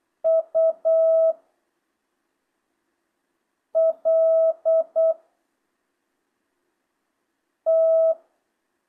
Morse code, heard as a keyed tone, about 8 wpm, 650 Hz. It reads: U L T
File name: jh8cbh / 2025cw